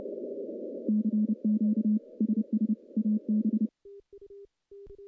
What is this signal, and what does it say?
Morse code, heard as a keyed tone, about 29 wpm, 220 Hz. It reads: CQ SS AB
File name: contests / CW-SS-k5tr- -1300